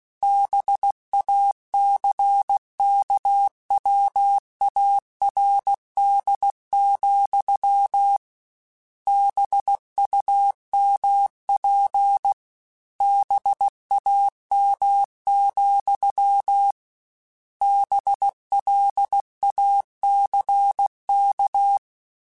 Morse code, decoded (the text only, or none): BACKWARD, BUMP BAM, BLACK